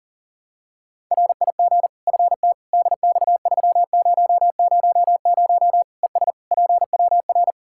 Morse code, heard as a keyed tone, about 40 wpm, 700 Hz. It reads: RIG FT DX3000 ES PWR
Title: Practice QSO - Exchange 2 - Info - 40wpm